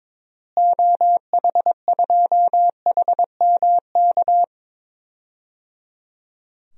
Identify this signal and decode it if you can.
Morse code, heard as a keyed tone, about 21 wpm, 700 Hz. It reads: OH2HMK
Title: DXCC-100 Call Signs - 6 Characters - Repeated 3x - 22wpm